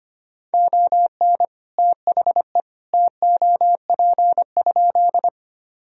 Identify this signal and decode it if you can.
Morse code, heard as a keyed tone, about 25 wpm, 700 Hz. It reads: ON THE TOP?
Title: Sentences from Top 1000 Words - Review Part 2 - 25wpm